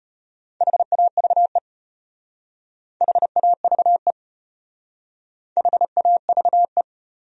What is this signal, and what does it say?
Morse code, heard as a keyed tone, about 34 wpm, 700 Hz. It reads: HAVE HAVE HAVE